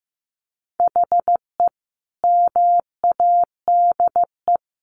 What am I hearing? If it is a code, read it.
Morse code, heard as a keyed tone, about 14 wpm, 700 Hz. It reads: HE MADE
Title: Sentences from Top 1100 Words 15wpm